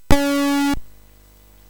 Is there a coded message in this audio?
No.